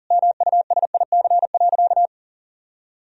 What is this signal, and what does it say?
Morse code, heard as a keyed tone, about 40 wpm, 700 Hz. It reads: MUSIC.
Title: Sentences from Top 1000 Words - Review Part 1 - 40wpm